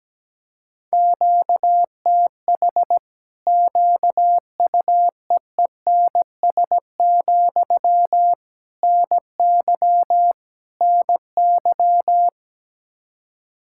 Morse code, heard as a keyed tone, about 17 wpm, 700 Hz. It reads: QTH QUEENS, NY NY
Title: Practice QSO - Exchange 1 - Beginning - 17wpm